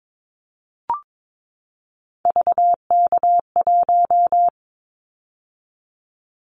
Morse code, heard as a keyed tone, about 22 wpm, 700 Hz. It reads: VK1
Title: Callsign Prefixes for Top 50 DXCC - 22wpm